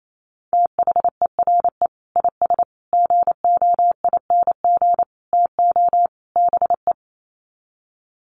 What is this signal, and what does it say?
Morse code, heard as a keyed tone, about 28 wpm, 700 Hz. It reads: THERE IS GOING TO BE
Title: Sets of 5 Words Encore 28wpm